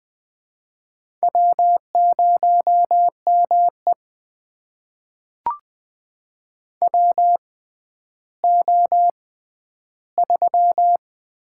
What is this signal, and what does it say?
Morse code, heard as a keyed tone, about 20 wpm, spaced as for 11 wpm, 700 Hz. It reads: W0ME WO3
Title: USA Call Signs - 4 Characters - 6x Character Spacing - 20wpm